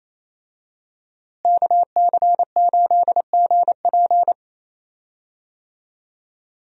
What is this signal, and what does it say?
Morse code, heard as a keyed tone, about 28 wpm, 700 Hz. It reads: KC8GP
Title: USA Call Signs - 5 Characters - Repeated 3x - 28wpm